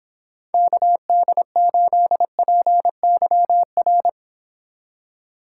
Morse code, heard as a keyed tone, about 29 wpm, 700 Hz. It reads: KD8PYR